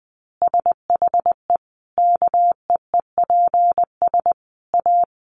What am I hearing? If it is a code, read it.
Morse code, heard as a keyed tone, about 20 wpm, 700 Hz. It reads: SHE KEEPS A
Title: Easy Phrases 20wpm